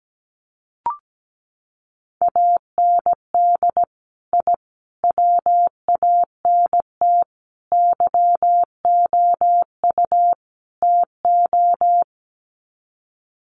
Morse code, heard as a keyed tone, about 17 wpm, 700 Hz. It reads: AND I WANT YOU TO
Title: Sets of 5 Words Encore 17wpm